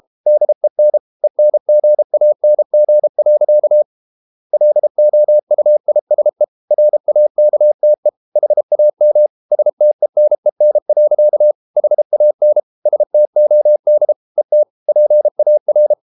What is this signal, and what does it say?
Morse code, heard as a keyed tone, about 34 wpm, 600 Hz. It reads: DEN ENGANG. LOUISE RAKTE HAM STENEN. HAN STOD ET PAR
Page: Ferie 32 wpm